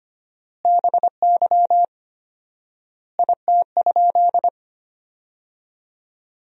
Morse code, heard as a keyed tone, about 25 wpm, 700 Hz. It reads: BY IT?
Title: 4x Word-Spacing - Sentences from Top 200 Words - 25wpm